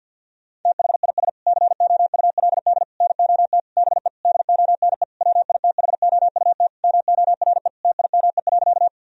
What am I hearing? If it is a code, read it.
Morse code, heard as a keyed tone, about 48 wpm, 700 Hz. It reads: THIS COULD NOT BE DONE WITHOUT MORE TIME.
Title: Sentences from Top 1000 Words - Review Part 3 - 50wpm